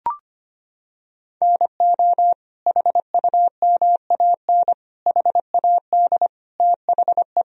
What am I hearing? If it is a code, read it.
Morse code, heard as a keyed tone, about 25 wpm, 700 Hz. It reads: NO HUMAN HAD THE
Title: Sentences from Top 1000 Words - Review Part 3 - 25wpm